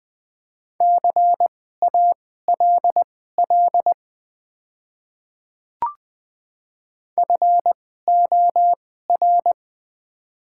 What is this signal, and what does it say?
Morse code, heard as a keyed tone, about 20 wpm, spaced as for 15 wpm, 700 Hz. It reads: CALL FOR
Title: 2x Character Spacing - Top 300 Words - 20wpm